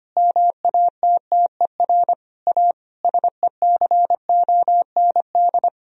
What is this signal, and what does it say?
Morse code, heard as a keyed tone, about 25 wpm, 700 Hz. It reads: MATTER A SECOND
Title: Sentences from Top 1000 Words - Review Part 2 - 25wpm